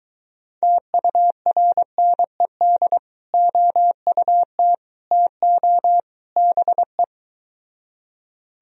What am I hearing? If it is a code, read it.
Morse code, heard as a keyed tone, about 23 wpm, 700 Hz. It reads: TURNED OUT TO BE